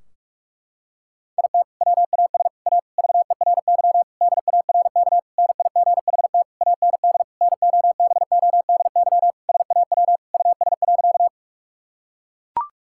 Morse code, heard as a keyed tone, about 41 wpm, 700 Hz. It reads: IT WAS A VERY DARK NIGHT AND NOBODY SAW US.
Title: Sentences from Top 1600 Words 45wpm